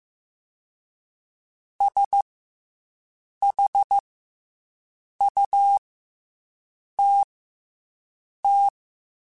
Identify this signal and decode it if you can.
Morse code, heard as a keyed tone, about 15 wpm, 800 Hz. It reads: S H U T T
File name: Special INGs / Special INGs-mp3s / SlowToFaster ing @13